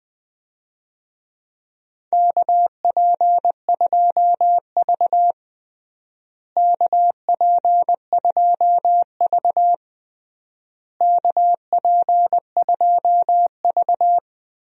Morse code, heard as a keyed tone, about 20 wpm, 700 Hz. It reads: KP2V KP2V KP2V
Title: DXCC-100 Call Signs - 4 Characters - Repeated 3x - 20wpm